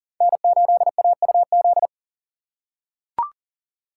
Morse code, heard as a keyed tone, about 38 wpm, 700 Hz. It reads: N8AUZ
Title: USA Call Signs - 5 Characters - 6x Character Spacing - 40wpm